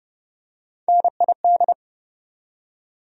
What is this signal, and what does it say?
Morse code, heard as a keyed tone, about 30 wpm, 700 Hz. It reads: NID
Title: USA Call Sign Postfixes - 3 Letters 30wpm